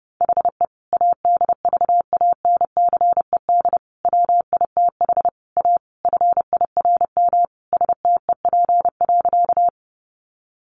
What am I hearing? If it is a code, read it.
Morse code, heard as a keyed tone, about 30 wpm, 700 Hz. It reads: HE ADVANCED WITH A FIRM STEP.
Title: Sentences from Top 1400 Words 30wpm